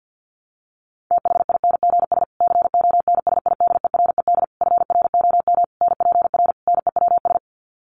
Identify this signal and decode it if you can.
Morse code, heard as a keyed tone, about 50 wpm, 700 Hz. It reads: THINGS CONSIDERED FROM NPR NEWS
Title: Sets of 5 Words Encore 50wpm